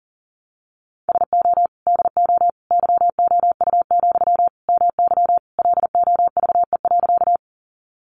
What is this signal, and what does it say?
Morse code, heard as a keyed tone, about 40 wpm, 700 Hz. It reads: SO DO YOU, MY LOVE.
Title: Sentences from Top 400 Words 40wpm